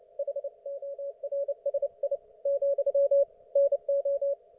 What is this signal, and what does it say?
Morse code, heard as a keyed tone, about 29 wpm, 560 Hz. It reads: HORSI , NO